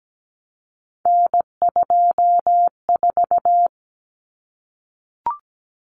Morse code, heard as a keyed tone, about 17 wpm, 700 Hz. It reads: N24
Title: Common Cut Numbers - 3-Digit 17wpm